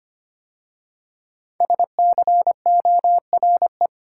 Morse code, heard as a keyed tone, about 25 wpm, 700 Hz. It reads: SCORE
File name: 5 Letter Words - First 3 Letters Given - 25wpm